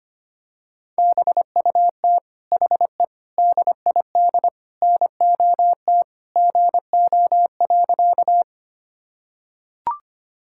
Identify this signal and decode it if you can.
Morse code, heard as a keyed tone, about 25 wpm, 700 Hz. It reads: BUT HE DID NOT GO.
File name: practice / 25wpm / Sentences with 4-Word Phrases - 25wpm